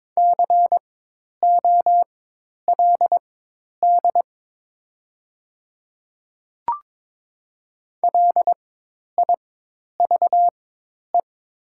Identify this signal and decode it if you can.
Morse code, heard as a keyed tone, about 23 wpm, spaced as for 10 wpm, 700 Hz. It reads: COLD LIVE
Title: 4x Character Spacing - Top 500 Words - 22wpm